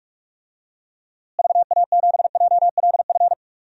Morse code, heard as a keyed tone, about 46 wpm, 700 Hz. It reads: VA7JLF
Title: DXCC-100 Call Signs - 6 Characters - Repeated 2x - 45wpm